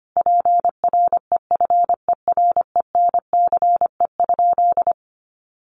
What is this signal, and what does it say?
Morse code, heard as a keyed tone, about 25 wpm, 700 Hz. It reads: PREFERENCE?
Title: Sentences from Top 1500 Words 25wpm